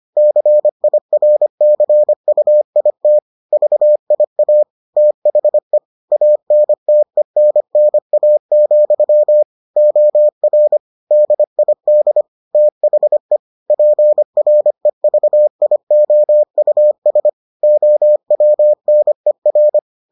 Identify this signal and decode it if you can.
Morse code, heard as a keyed tone, about 25 wpm, 600 Hz. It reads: CIRCUIT VIA THE ANTENNA, OR DID THE PREVIOUS OWNER